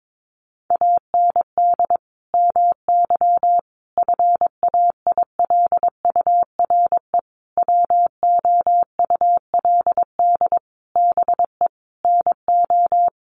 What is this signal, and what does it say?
Morse code, heard as a keyed tone, about 22 wpm, 700 Hz. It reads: AND MY FAILURE WOULD BE NO
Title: Sentences from Top 1400 Words 22wpm